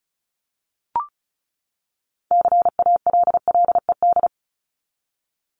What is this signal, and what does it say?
Morse code, heard as a keyed tone, about 34 wpm, 700 Hz. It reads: CALLED
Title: ED Suffix - Top 100 Words with Base Words 35wpm